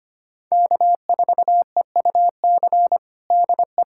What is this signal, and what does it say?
Morse code, heard as a keyed tone, about 25 wpm, 700 Hz. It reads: K4EUC DE
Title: USA Callsign DE Callsign - 5 Characters - 25wpm